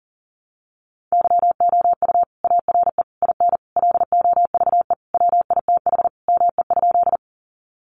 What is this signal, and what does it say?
Morse code, heard as a keyed tone, about 40 wpm, 700 Hz. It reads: YOU ARE IN LOVE WITH ME?
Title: Sentences from Top 400 Words 40wpm